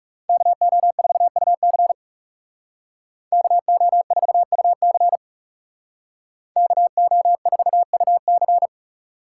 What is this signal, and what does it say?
Morse code, heard as a keyed tone, about 39 wpm, 700 Hz. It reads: KO4UC KO4UC KO4UC